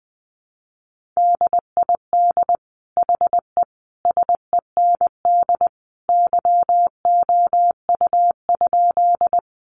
Morse code, heard as a keyed tone, about 20 wpm, 700 Hz. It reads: DID HE SEND YOU?
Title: Sentences from Top 1500 Words 20wpm